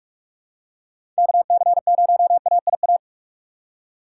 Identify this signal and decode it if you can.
Morse code, heard as a keyed tone, about 46 wpm, 700 Hz. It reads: KC0AIA
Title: USA Call Signs - 6 Characters - 2x Character Spacing - 45wpm